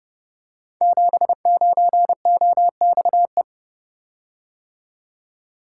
Morse code, heard as a keyed tone, about 30 wpm, 700 Hz. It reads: 79OXE
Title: Code Groups - 5 Letter:Numbers - 30wpm